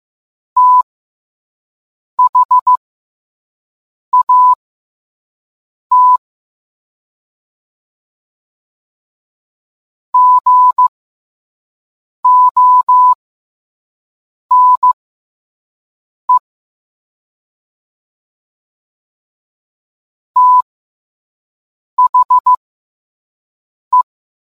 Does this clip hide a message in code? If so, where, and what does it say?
Yes: Morse code from 0.6 to 24.0 s, heard as a keyed tone, about 15 wpm, spaced as for 5 wpm, 1000 Hz. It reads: THAT GONE THE